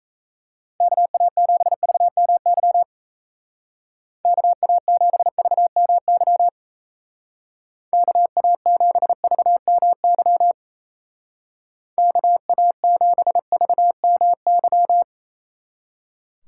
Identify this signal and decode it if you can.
Morse code, heard as a keyed tone, about 35 wpm, 700 Hz. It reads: KA7VMY KA7VMY KA7VMY KA7VMY
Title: Speed Racing - USA Call Signs 28wpm